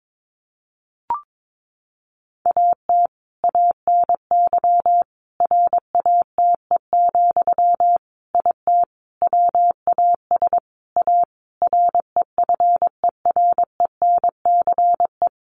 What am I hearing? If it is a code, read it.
Morse code, heard as a keyed tone, about 22 wpm, 700 Hz. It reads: AT ANY RATE, IT WAS A REFERENCE
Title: Sentences from Top 1200 Words 22wpm